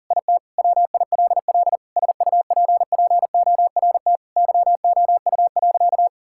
Morse code, heard as a keyed tone, about 37 wpm, 700 Hz. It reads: IT WILL SUPPORT YOU.
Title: Sentences from Top 1000 Words - Review Part 5 - 40wpm